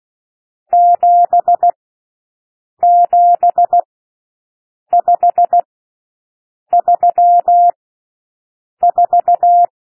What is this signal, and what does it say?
Morse code, heard as a keyed tone, about 16 wpm, spaced as for 6 wpm, 700 Hz. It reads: 77534
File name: SOUND NUMBERS 16 wpm